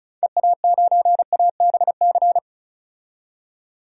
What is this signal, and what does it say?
Morse code, heard as a keyed tone, about 34 wpm, 700 Hz. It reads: EA9ABC
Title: DXCC-100 Call Signs - 6 Characters - 35wpm